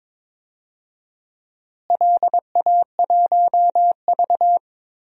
Morse code, heard as a keyed tone, about 22 wpm, 700 Hz. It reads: LA1V
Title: DXCC-100 Call Signs - 4 Characters - Repeated 3x - 22wpm